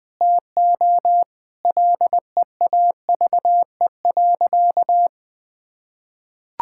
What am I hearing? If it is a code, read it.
Morse code, heard as a keyed tone, about 20 wpm, 700 Hz. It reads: TO LEAVE.
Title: Sentences with 5-Word Phrases - 20wpm